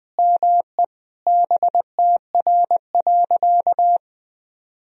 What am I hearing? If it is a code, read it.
Morse code, heard as a keyed tone, about 21 wpm, 700 Hz. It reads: ME BTR.
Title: Sentences from Top 400 Words and Abbreviations 20wpm